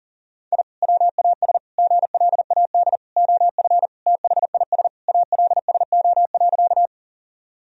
Morse code, heard as a keyed tone, about 40 wpm, 700 Hz. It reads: I WAS GLAD OF THIS ALSO.
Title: Sentences from Top 1000 Words - Review Part 1 - 40wpm